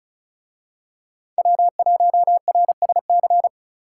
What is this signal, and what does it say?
Morse code, heard as a keyed tone, about 33 wpm, 700 Hz. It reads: W1RSC
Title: Speed Racing - USA 1x3 Call Signs 35wpm